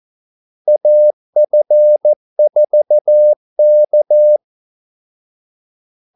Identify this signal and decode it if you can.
Morse code, heard as a keyed tone, about 14 wpm, 600 Hz. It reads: AF4K
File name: Callsigns-14wpm